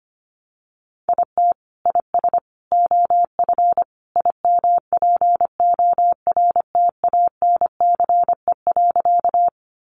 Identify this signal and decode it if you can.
Morse code, heard as a keyed tone, about 25 wpm, 700 Hz. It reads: IT IS OF IMPORTANCE.
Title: Sentences from Top 1200 Words 25wpm